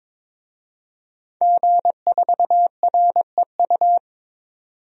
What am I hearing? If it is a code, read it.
Morse code, heard as a keyed tone, about 21 wpm, 700 Hz. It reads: G4REU
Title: DXCC-100 Call Signs - 5 Characters - 22wpm